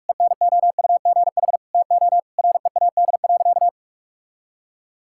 Morse code, heard as a keyed tone, about 46 wpm, 700 Hz. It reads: ENOUGH TO READ.
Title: Sentences from Top 1000 Words - Review Part 5 - 45wpm